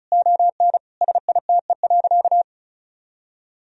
Morse code, heard as a keyed tone, about 36 wpm, 700 Hz. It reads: ON SITE.
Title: Sentences from Top 1000 Words - Review Part 4 - 35wpm